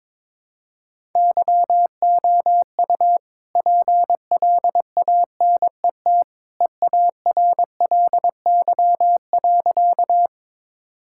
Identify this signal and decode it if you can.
Morse code, heard as a keyed tone, about 21 wpm, 690 Hz. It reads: YOU PLANET EARLY.
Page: morse-code-ninja-practice-22wpm